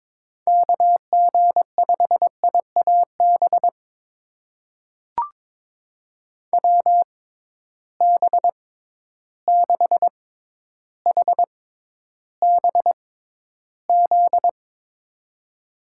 Morse code, heard as a keyed tone, about 22 wpm, spaced as for 11 wpm, 700 Hz. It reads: KG5IAB WB6HBZ
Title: USA Call Signs - 6 Characters - 6x Character Spacing - 22wpm